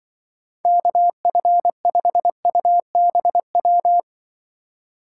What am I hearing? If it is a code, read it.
Morse code, heard as a keyed tone, about 24 wpm, 700 Hz. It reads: KF5UBW
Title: Speed Racing - USA 2x3 Call Signs 20wpm